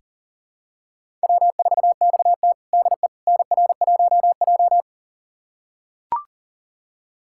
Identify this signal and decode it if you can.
Morse code, heard as a keyed tone, about 40 wpm, 700 Hz. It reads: W4XT DE NR1J
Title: USA Callsign DE Callsign - 4 Characters - 40wpm